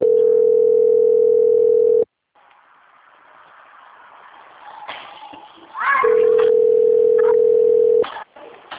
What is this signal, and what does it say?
Ringback tone (North America), 440 + 480 Hz.